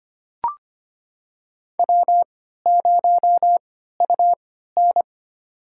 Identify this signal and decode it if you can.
Morse code, heard as a keyed tone, about 25 wpm, spaced as for 14 wpm, 700 Hz. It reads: W0UN
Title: USA Call Signs - 4 Characters - 3x Character Spacing - 25wpm